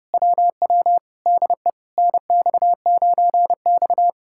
Morse code, heard as a keyed tone, about 33 wpm, 700 Hz. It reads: WW DE NX9X